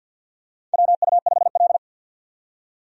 Morse code, heard as a keyed tone, about 47 wpm, 700 Hz. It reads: WU5L
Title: USA Call Signs - 4 Characters - 50wpm